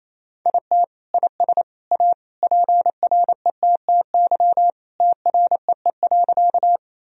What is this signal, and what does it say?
Morse code, heard as a keyed tone, about 28 wpm, 700 Hz. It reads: IT IS A PRETTY TREE.
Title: Sentences from Top 1000 Words - Review Part 1 - 28wpm